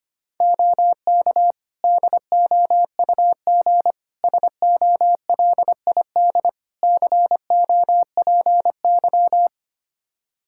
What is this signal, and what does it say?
Morse code, heard as a keyed tone, about 25 wpm, 700 Hz. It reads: OK DOUG SOLID COPY
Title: K1USN SST QSO Practice 25wpm